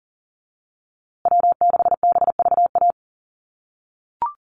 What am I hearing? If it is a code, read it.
Morse code, heard as a keyed tone, about 40 wpm, 700 Hz. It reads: W6BVA
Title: USA Call Signs Repeated 2x 40wpm